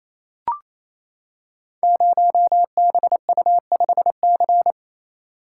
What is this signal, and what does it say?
Morse code, heard as a keyed tone, about 29 wpm, 700 Hz. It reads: 0BU5C
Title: Code Groups - 5 Letter:Numbers - 28wpm